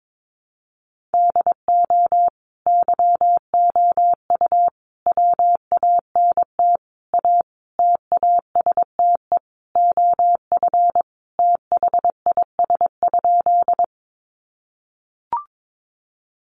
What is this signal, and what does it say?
Morse code, heard as a keyed tone, about 22 wpm, 700 Hz. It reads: DO YOU WANT A TASTE OF THIS?
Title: Sentences from Top 1100 Words 22wpm